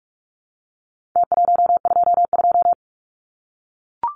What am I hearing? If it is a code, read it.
Morse code, heard as a keyed tone, about 44 wpm, 700 Hz. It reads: T122
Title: Numbers - Contest Style - 4-Digit 45wpm